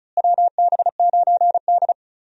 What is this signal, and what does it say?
Morse code, heard as a keyed tone, about 36 wpm, 700 Hz. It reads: WB9D
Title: USA Call Signs - 4 Characters - Repeated 2x - 35wpm